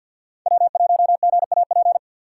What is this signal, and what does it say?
Morse code, heard as a keyed tone, about 52 wpm, 700 Hz. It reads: W1GAP
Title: USA Call Signs - 5 Characters - 4x Character Spacing - 50wpm